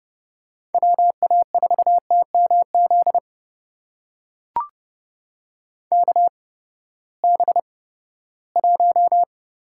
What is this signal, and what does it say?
Morse code, heard as a keyed tone, about 30 wpm, spaced as for 17 wpm, 700 Hz. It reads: WA4TMZ KB1
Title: USA Call Signs - 6 Characters - 8x Character Spacing - 30wpm